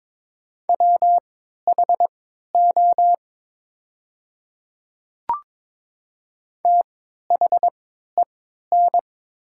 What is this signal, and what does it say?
Morse code, heard as a keyed tone, about 21 wpm, spaced as for 12 wpm, 700 Hz. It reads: WHO THEN
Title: 3x Character Spacing - Top 100 Words - 22wpm